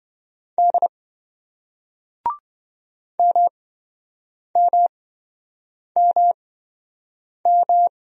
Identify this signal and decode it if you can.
Morse code, heard as a keyed tone, about 26 wpm, spaced as for 7 wpm, 700 Hz. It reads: D MMMM